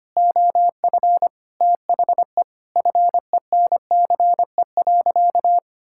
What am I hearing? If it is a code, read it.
Morse code, heard as a keyed tone, about 25 wpm, 700 Hz. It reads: OF THE FENCE.
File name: practice / 25wpm / Sentences with 4-Word Phrases - 25wpm